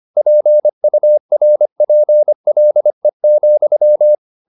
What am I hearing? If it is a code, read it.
Morse code, heard as a keyed tone, about 25 wpm, 600 Hz. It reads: PURPLE,
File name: the old man and the sea part 1 25wpm